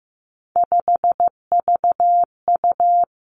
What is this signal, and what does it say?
Morse code, heard as a keyed tone, about 15 wpm, 700 Hz. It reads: 5VU
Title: Three Challenging Letter-Number Combinations 15wpm